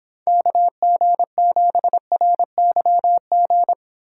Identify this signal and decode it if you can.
Morse code, heard as a keyed tone, about 30 wpm, 700 Hz. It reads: KG7RYG